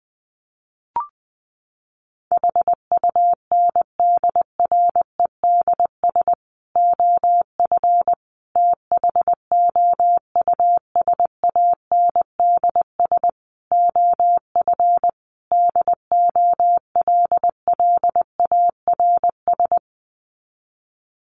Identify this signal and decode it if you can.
Morse code, heard as a keyed tone, about 19 wpm, 700 Hz. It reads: HUNDREDS OF THOUSANDS OF DOLLARS
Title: Sets of 5 Words Encore 20wpm